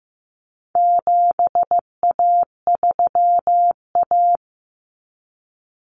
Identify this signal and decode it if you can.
Morse code, heard as a keyed tone, about 15 wpm, 700 Hz. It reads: 7A3A
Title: Common Cut Numbers - 4-Digit 15wpm